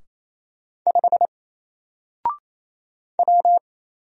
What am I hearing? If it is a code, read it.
Morse code, heard as a keyed tone, about 27 wpm, 700 Hz. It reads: 5 W
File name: Single-Letter-Number-28wpm